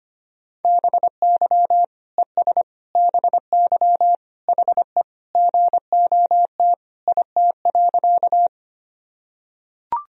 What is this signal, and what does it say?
Morse code, heard as a keyed tone, about 25 wpm, 700 Hz. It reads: BY ES BY HE GOT IT.
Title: Sentences from Top 300 Words and Abbreviations 25wpm